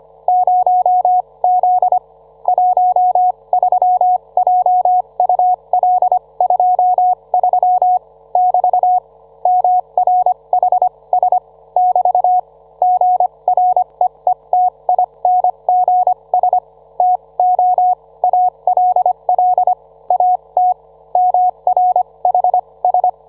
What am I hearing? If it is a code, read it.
Morse code, heard as a keyed tone, about 25 wpm, spaced as for 21 wpm, 730 Hz. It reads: 0Z 13JUL23 <BT> MRHS <BT> GREETINGS TO ALL AT MRHS